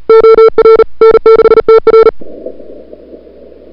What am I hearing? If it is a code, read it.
Morse code, heard as a keyed tone, about 38 wpm, 440 Hz. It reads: OR N6TR
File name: CWSprint / February 2007 CW Sprint / 01-2007 CW Sprint Hour1